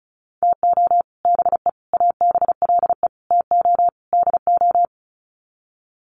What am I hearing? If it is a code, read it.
Morse code, heard as a keyed tone, about 34 wpm, 700 Hz. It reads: TO BE ABLE TO DO
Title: Sets of 5 Words 35wpm